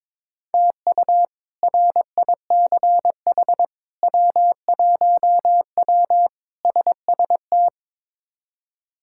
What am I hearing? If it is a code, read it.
Morse code, heard as a keyed tone, about 22 wpm, 700 Hz. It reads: TU RICH W1W SST
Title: K1USN SST QSO Practice - 22wpm